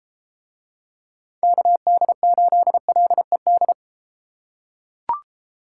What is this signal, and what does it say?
Morse code, heard as a keyed tone, about 33 wpm, 700 Hz. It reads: KD8LED